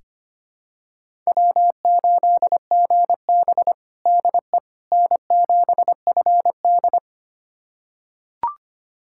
Morse code, heard as a keyed tone, about 25 wpm, 700 Hz. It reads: W8GB DE N7FD
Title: USA Callsign DE Callsign - 4 Characters - 25wpm